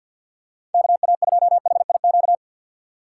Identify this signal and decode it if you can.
Morse code, heard as a keyed tone, about 53 wpm, 700 Hz. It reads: KA2HIQ
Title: USA Call Signs - 6 Characters - 6x Character Spacing - 50wpm